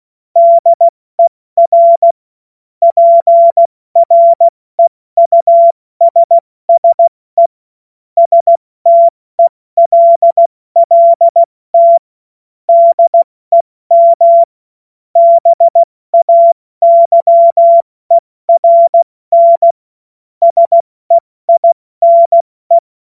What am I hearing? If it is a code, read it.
Morse code, heard as a keyed tone, about 16 wpm, 680 Hz. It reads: DER PREUSSE STELLT DEM BAYERN SEINE
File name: Witz-26-August-2025-70-F80